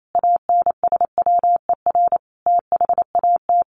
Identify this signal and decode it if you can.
Morse code, heard as a keyed tone, about 28 wpm, 700 Hz. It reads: ANSWER THAT
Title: Sentences from Top 400 Words 28wpm